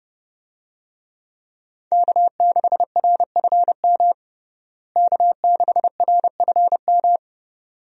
Morse code, heard as a keyed tone, about 32 wpm, 700 Hz. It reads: K6RFM K6RFM
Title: USA Call Signs - 5 Characters - Repeated 2x - 30wpm